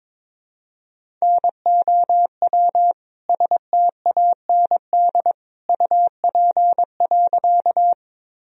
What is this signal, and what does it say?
Morse code, heard as a keyed tone, about 22 wpm, 700 Hz. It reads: NOW STAND UP.
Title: Sentences from Top 1000 Words - Review Part 1 - 22wpm